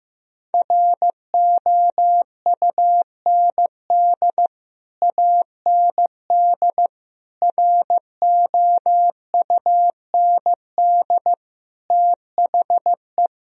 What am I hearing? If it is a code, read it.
Morse code, heard as a keyed tone, about 16 wpm, 700 Hz. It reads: ROUND AND ROUND THE
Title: Sentences from Top 1000 Words - Review Part 3 - 15wpm